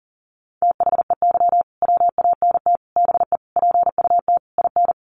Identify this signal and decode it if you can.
Morse code, heard as a keyed tone, about 37 wpm, 700 Hz. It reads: THEY WANT BE PUT IN
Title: Sentences from Top 1000 Words 40wpm